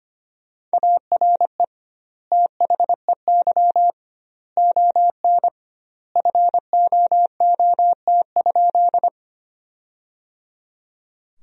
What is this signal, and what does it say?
Morse code, heard as a keyed tone, about 25 wpm, 700 Hz. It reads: ARE THEY ON FOOT?
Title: 2x Word-Spacing - Sentences from Top 500 Words - 25wpm